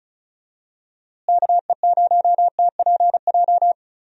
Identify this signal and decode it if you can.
Morse code, heard as a keyed tone, about 34 wpm, 700 Hz. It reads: KE0TPJ
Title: USA Call Signs - 6 Characters - Repeated 2x - 35wpm